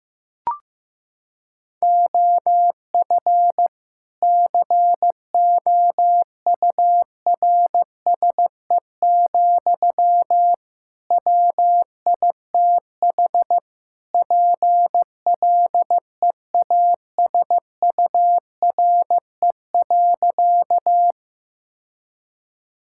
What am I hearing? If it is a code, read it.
Morse code, heard as a keyed tone, about 15 wpm, 700 Hz. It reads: OF COURSE, WITH PLEASURE.
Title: Sentences from Top 1600 Words 15wpm